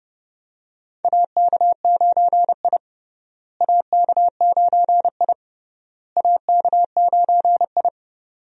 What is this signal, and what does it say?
Morse code, heard as a keyed tone, about 32 wpm, 700 Hz. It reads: AK9I AK9I AK9I
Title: USA Call Signs - 4 Characters - Repeated 3x - 30wpm